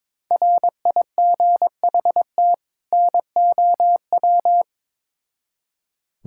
Morse code, heard as a keyed tone, about 22 wpm, 700 Hz. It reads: RIGHT NOW
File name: Build A Phrase - 2-Word Phrases - 22wpm